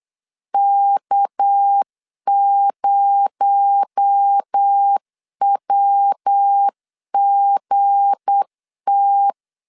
Morse code, heard as a keyed tone, about 8 wpm, 800 Hz. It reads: K0WGT
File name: LEZIONI / 7 GF9Z / 7 G 8wpm 3118